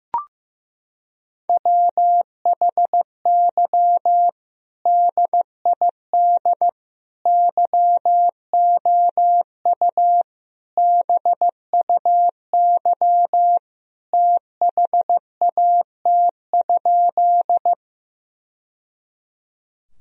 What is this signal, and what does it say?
Morse code, heard as a keyed tone, about 15 wpm, 700 Hz. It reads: WHY DID YOU BUY THAT?
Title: Sentences from Top 1000 Words - Review Part 4 - 15wpm